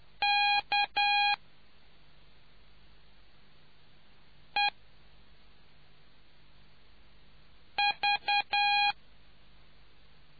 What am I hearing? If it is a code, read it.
Morse code, heard as a keyed tone, about 10 wpm, spaced as for 3 wpm, 820 Hz. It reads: KEV